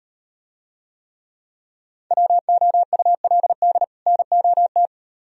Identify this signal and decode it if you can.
Morse code, heard as a keyed tone, about 38 wpm, 700 Hz. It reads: WOULD NOT